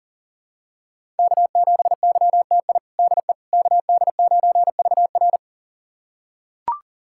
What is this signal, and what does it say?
Morse code, heard as a keyed tone, about 40 wpm, 700 Hz. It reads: K7YTI DE KD9VR
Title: USA Callsign DE Callsign - 5 Characters - 40wpm